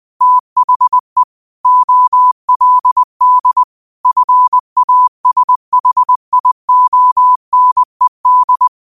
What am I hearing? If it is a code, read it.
Morse code, heard as a keyed tone, about 20 wpm, 1000 Hz. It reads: THE OLD FASHIONED
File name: qotdfiles / 20wpm